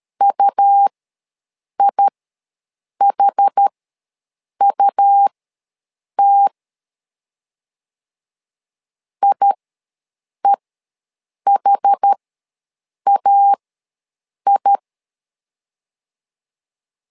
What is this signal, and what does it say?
Morse code, heard as a keyed tone, about 13 wpm, spaced as for 7 wpm, 800 Hz. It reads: UIHUT IEHAI
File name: LEZIONI / 2 UHDN / 2 UH ws 12wpm 1730